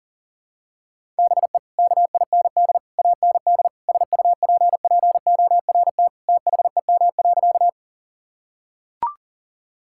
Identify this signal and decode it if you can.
Morse code, heard as a keyed tone, about 40 wpm, 700 Hz. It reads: BE KIND AND SUPPORT THEM.
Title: Sentences from Top 1000 Words - Review Part 5 - 40wpm